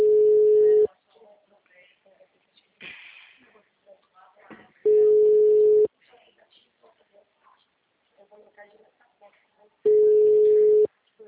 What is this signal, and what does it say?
Ringback tone (Europe, 425 Hz), with speech.
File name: out- -2007- - - .82422